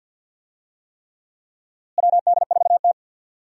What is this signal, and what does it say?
Morse code, heard as a keyed tone, about 47 wpm, 700 Hz. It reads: WD4T
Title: USA Call Signs - 4 Characters - Repeated 3x - 50wpm